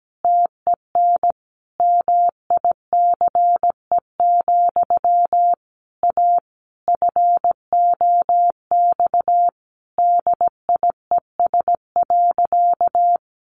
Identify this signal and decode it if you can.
Morse code, heard as a keyed tone, about 17 wpm, 700 Hz. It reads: TEN MICE, A FOX DIES.
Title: A Sound Of Thunder 17wpm